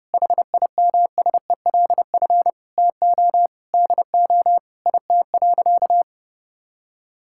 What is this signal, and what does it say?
Morse code, heard as a keyed tone, about 30 wpm, 700 Hz. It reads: HIMSELF TO DO IT.
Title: Sentences from Top 1600 Words 30wpm